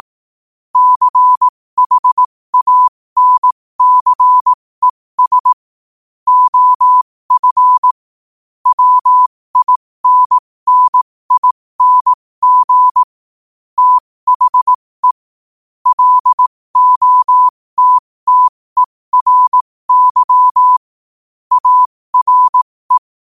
Morse code, heard as a keyed tone, about 18 wpm, spaced as for 15 wpm, 1000 Hz. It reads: CHANCES OF WINNING THE LOTTERY ARE
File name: qotdfiles / 15wpm